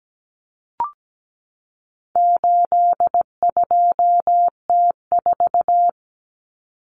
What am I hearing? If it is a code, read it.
Morse code, heard as a keyed tone, about 17 wpm, 700 Hz. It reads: 82T4
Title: Common Cut Numbers - 4-Digit 17wpm